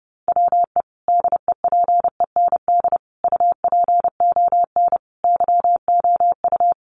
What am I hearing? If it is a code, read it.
Morse code, heard as a keyed tone, about 30 wpm, 700 Hz. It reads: WE DEPEND UPON YOU
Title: Sentences from Top 1300 Words 30wpm